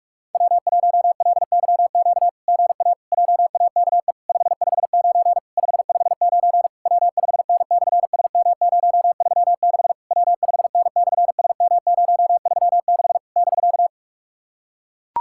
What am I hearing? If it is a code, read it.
Morse code, heard as a keyed tone, about 46 wpm, 700 Hz. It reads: W1LYQ GA JAKE 559 559 W5N/SM036 W5N/SM036 <BK>
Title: SOTA QSO Practice 45wpm